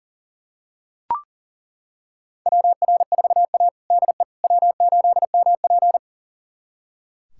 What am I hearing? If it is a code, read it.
Morse code, heard as a keyed tone, about 37 wpm, 700 Hz. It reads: WR4A DE W8MP
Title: USA Callsign DE Callsign - 4 Characters - 40wpm